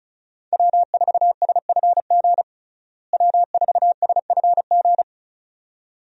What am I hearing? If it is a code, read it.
Morse code, heard as a keyed tone, about 34 wpm, 700 Hz. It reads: W4SFG W4SFG
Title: USA Call Signs - 5 Characters - Repeated 2x - 35wpm